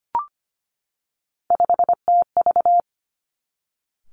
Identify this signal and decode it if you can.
Morse code, heard as a keyed tone, about 25 wpm, 700 Hz. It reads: 5TV
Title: Three Challenging Letter-Number Combinations 25wpm